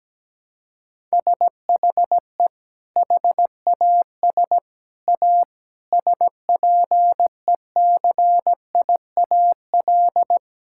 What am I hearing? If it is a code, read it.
Morse code, heard as a keyed tone, about 17 wpm, 700 Hz. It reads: SHE HAS A SPECIAL
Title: Sentences from Top 1000 Words - Review Part 5 - 17wpm